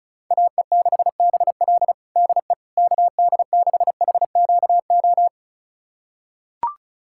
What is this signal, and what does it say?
Morse code, heard as a keyed tone, about 36 wpm, 700 Hz. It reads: AE6BL DE KD6HQO
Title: USA Callsign DE Callsign 35wpm